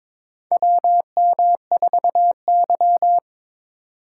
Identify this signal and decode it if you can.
Morse code, heard as a keyed tone, about 22 wpm, 700 Hz. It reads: WM4Y
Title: USA Call Signs - 3x Character Spacing - 22wpm